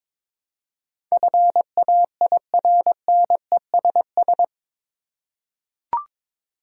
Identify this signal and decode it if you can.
Morse code, heard as a keyed tone, about 22 wpm, 700 Hz. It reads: FAIRNESS
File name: practice / 22wpm / Eight Letter Words - 22wpm